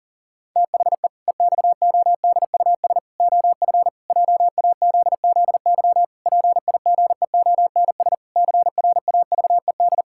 Morse code, heard as a keyed tone, about 40 wpm, 700 Hz. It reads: THE EXODUS OF JAZZY PIGEONS CRAVED
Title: Pangram Challenge v2 - 40wpm